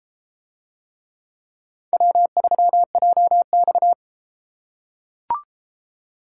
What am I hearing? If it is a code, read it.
Morse code, heard as a keyed tone, about 34 wpm, 700 Hz. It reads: W3JX